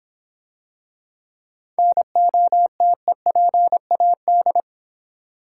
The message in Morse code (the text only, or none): NOTEPAD